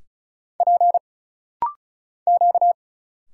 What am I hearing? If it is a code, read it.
Morse code, heard as a keyed tone, about 36 wpm, 700 Hz. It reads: P Q
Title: Single Letters - Rapid-Fire 35wpm